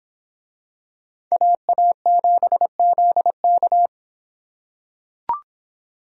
Morse code, heard as a keyed tone, about 24 wpm, 700 Hz. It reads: AA7ZK